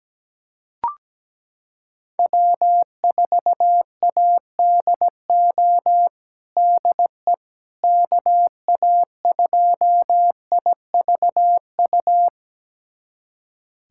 Morse code, heard as a keyed tone, about 17 wpm, 700 Hz. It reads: W4ADO DE KA2IVU
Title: USA Callsign DE Callsign 17wpm